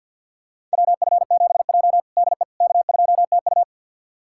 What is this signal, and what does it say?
Morse code, heard as a keyed tone, about 48 wpm, 700 Hz. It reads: WF7J DE K2TU
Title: USA Callsign DE Callsign - 4 Characters - 50wpm